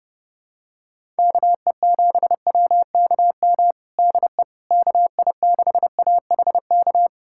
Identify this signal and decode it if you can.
Morse code, heard as a keyed tone, about 30 wpm, 700 Hz. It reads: KE7WKM DE KI6AHK
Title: USA Callsign DE Callsign - 6 Characters - 30wpm